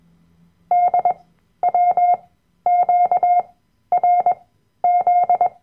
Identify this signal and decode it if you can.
Morse code, heard as a keyed tone, about 21 wpm, spaced as for 12 wpm, 710 Hz. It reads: DWQRZ
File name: Workshop Dateien / CW-Online-Kurs / Lektion 010